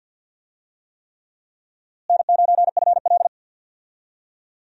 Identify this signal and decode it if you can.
Morse code, heard as a keyed tone, about 51 wpm, 700 Hz. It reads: N9FL
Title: USA Call Signs - 4 Characters - Repeated 3x - 50wpm